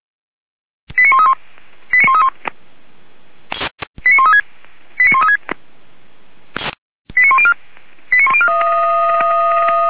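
Five-tone selective call (ZVEI), heard as quick runs of five tones, tones of about 70 ms. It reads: 89131 89131 89136 89136 89104 89104
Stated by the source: ZVEI five-tone call received by SDR, fire brigade Feuerwehr Orlamünde.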